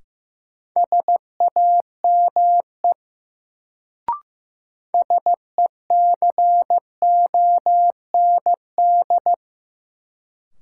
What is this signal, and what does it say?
Morse code, heard as a keyed tone, about 15 wpm, 700 Hz. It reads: SAME SECOND
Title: Sets of 2 Words - New Word Drill - 15wpm